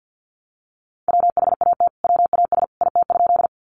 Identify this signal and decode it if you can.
Morse code, heard as a keyed tone, about 50 wpm, 700 Hz. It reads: WHAT WAS IT?
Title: Sentences from Top 100 Words 50wpm